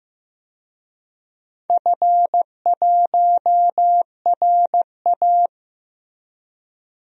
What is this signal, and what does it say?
Morse code, heard as a keyed tone, about 15 wpm, 700 Hz. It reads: F1RA
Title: DXCC-100 Call Signs - 4 Characters - Repeated 3x - 15wpm